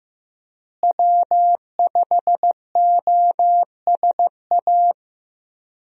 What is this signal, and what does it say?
Morse code, heard as a keyed tone, about 15 wpm, 700 Hz. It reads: W5OSA
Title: USA Call Signs - 5 Characters - Repeated 2x - 15wpm